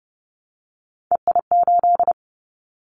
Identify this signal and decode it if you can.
Morse code, heard as a keyed tone, about 30 wpm, 700 Hz. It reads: EI8
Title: Callsign Prefixes for Top 50 DXCC - 30wpm